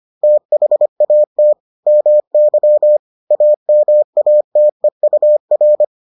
Morse code, heard as a keyed tone, about 25 wpm, 600 Hz. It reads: THAT MY AMATEUR